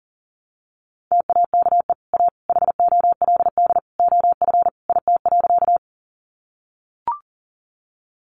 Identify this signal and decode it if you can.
Morse code, heard as a keyed tone, about 40 wpm, 700 Hz. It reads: TAKE A HOLD OF IT.
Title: Sentences from Top 400 Words 40wpm